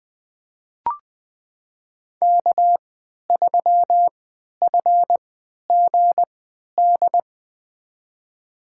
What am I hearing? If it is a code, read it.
Morse code, heard as a keyed tone, about 20 wpm, spaced as for 11 wpm, 700 Hz. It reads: K3FGD
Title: USA Call Signs - 3x Character Spacing - 20wpm